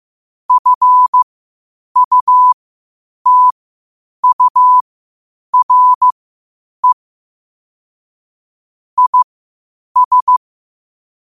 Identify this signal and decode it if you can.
Morse code, heard as a keyed tone, about 15 wpm, spaced as for 8 wpm, 1000 Hz. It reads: FUTURE IS
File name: qotdfiles / 8wpm